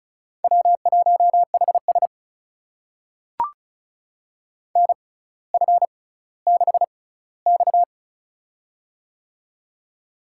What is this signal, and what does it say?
Morse code, heard as a keyed tone, about 36 wpm, spaced as for 18 wpm, 700 Hz. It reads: W1HS NF6X
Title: USA Call Signs - 4 Characters - 6x Character Spacing - 35wpm